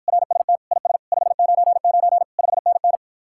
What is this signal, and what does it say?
Morse code, heard as a keyed tone, about 54 wpm, 700 Hz. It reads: RST IS 599 5NN